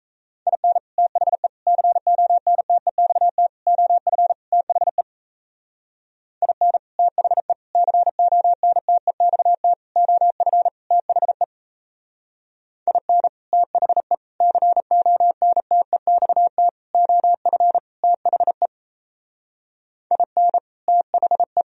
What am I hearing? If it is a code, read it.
Morse code, heard as a keyed tone, about 34 wpm, 700 Hz. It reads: IN THE CONTEXT OF THE IN THE CONTEXT OF THE IN THE CONTEXT OF THE IN THE
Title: Speed-Racing - Sets of 5 Words 28wpm